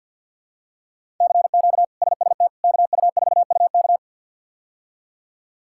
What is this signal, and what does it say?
Morse code, heard as a keyed tone, about 53 wpm, 700 Hz. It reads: CQ SST KU4UK
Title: K1USN SST QSO Practice - 50wpm